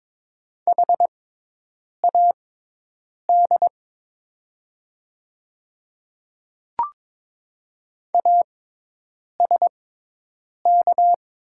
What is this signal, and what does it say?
Morse code, heard as a keyed tone, about 22 wpm, spaced as for 8 wpm, 700 Hz. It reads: HAD ASK
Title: 6x Character Spacing - Top 200 Words - 22wpm